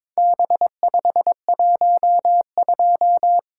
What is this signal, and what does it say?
Morse code, heard as a keyed tone, about 22 wpm, 700 Hz. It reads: B512
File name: practice / 22wpm / Four Letter-Number Combinations - 22wpm